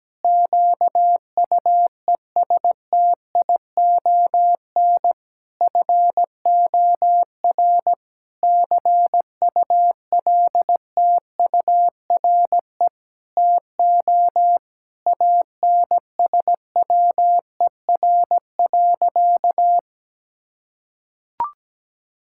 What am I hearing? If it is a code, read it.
Morse code, heard as a keyed tone, about 17 wpm, 700 Hz. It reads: QUESTION FOR CULTURE TO ANSWER.
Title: Sentences from Top 1000 Words - Review Part 6 - 17wpm